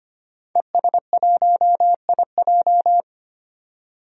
Morse code, heard as a keyed tone, about 25 wpm, 700 Hz. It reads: ES1IJ
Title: DXCC-100 Call Signs - 5 Characters - 25wpm